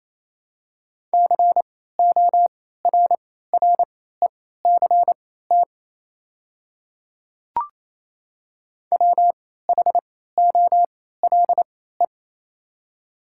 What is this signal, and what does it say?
Morse code, heard as a keyed tone, about 27 wpm, spaced as for 16 wpm, 700 Hz. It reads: CORRECT WHOLE
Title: 3x Character Spacing - Top 500 Words - 28wpm